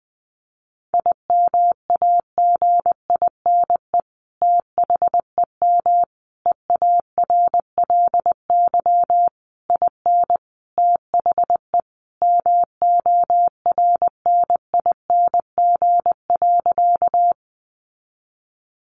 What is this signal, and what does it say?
Morse code, heard as a keyed tone, about 20 wpm, 700 Hz. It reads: IMAGINE THEM EARLY IN THE MORNING.
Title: Sentences from Top 1400 Words 20wpm